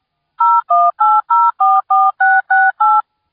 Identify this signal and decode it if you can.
DTMF, heard as telephone touch tones, tones of about 220 ms. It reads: 018044668